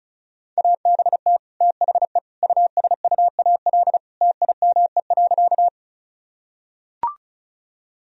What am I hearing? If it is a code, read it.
Morse code, heard as a keyed tone, about 34 wpm, 700 Hz. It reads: ABT THE USUAL TIME.
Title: Sentences from Top 400 Words and Abbreviations 35wpm